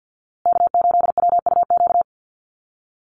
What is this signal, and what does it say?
Morse code, heard as a keyed tone, about 50 wpm, 700 Hz. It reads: K8WUQ
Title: USA Call Signs Repeated 3x 50wpm